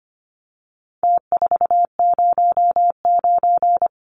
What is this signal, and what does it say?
Morse code, heard as a keyed tone, about 25 wpm, 700 Hz. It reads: T409
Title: Numbers - Contest Style - 4-Digit 25wpm